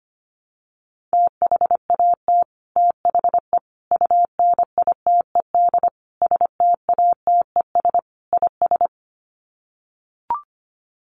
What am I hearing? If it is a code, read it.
Morse code, heard as a keyed tone, about 25 wpm, 700 Hz. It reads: THAT THE UNITED STATES IS
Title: Sets of 5 Words Encore 25wpm